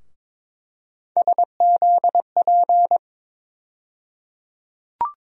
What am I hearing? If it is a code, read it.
Morse code, heard as a keyed tone, about 22 wpm, 700 Hz. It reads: SZP
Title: USA Call Sign Postfixes - 3 Letters 22wpm